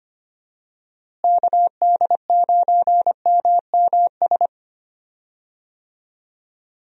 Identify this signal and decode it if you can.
Morse code, heard as a keyed tone, about 25 wpm, 700 Hz. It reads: KD9MMS
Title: USA Call Signs - 6 Characters - Repeated 3x - 25wpm